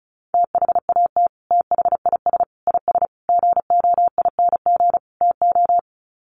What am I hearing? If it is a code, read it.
Morse code, heard as a keyed tone, about 34 wpm, 700 Hz. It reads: THAT THIS IS GOING TO
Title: Sets of 5 Words Encore2 35wpm